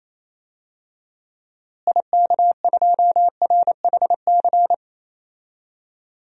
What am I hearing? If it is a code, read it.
Morse code, heard as a keyed tone, about 28 wpm, 700 Hz. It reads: IK2RHC
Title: DXCC-100 Call Signs - 6 Characters - Repeated 2x - 28wpm